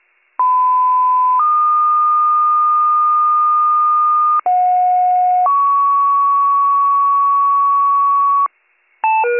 Two-tone page, heard as a one-second tone, then a three-second tone, about 1010 then 1220 Hz, about 730 then 1120 Hz. It is heard from 0.4 to 8.5 s.